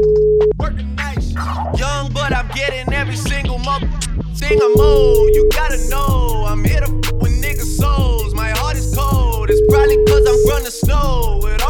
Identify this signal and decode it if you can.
Ringback tone (Europe, 425 Hz), with noise.